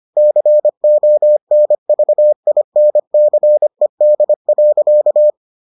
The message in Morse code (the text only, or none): CONVINCED.